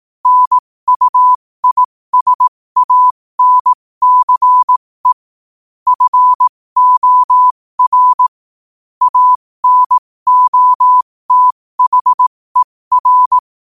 Morse code, heard as a keyed tone, about 18 wpm, spaced as for 15 wpm, 1000 Hz. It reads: NUISANCE FOR ANOTHER